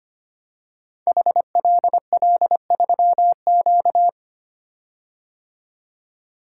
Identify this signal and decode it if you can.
Morse code, heard as a keyed tone, about 25 wpm, 700 Hz. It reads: HLL3Q
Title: Code Groups - 5 Letter:Numbers - 25wpm